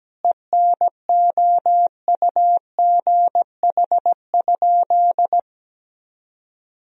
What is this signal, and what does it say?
Morse code, heard as a keyed tone, about 17 wpm, 700 Hz. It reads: ENOUGH?
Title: Sentences from Top 1000 Words - Review Part 3 - 17wpm